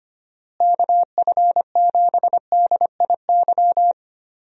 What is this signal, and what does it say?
Morse code, heard as a keyed tone, about 25 wpm, 700 Hz. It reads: KF7DIY
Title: USA Call Signs - 6 Characters - Repeated 2x - 25wpm